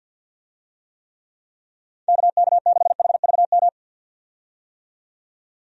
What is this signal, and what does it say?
Morse code, heard as a keyed tone, about 50 wpm, 700 Hz. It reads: KK6HVM
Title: USA Call Signs - 6 Characters - 50wpm